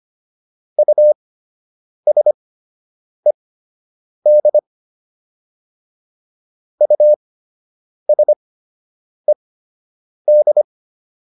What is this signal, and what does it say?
Morse code, heard as a keyed tone, about 25 wpm, spaced as for 8 wpm, 600 Hz. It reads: USED USED
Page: fundamental-practice-files